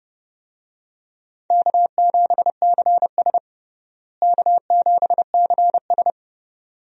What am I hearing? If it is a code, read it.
Morse code, heard as a keyed tone, about 31 wpm, 700 Hz. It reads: K7CS K7CS
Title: USA Call Signs - 4 Characters - Repeated 2x - 30wpm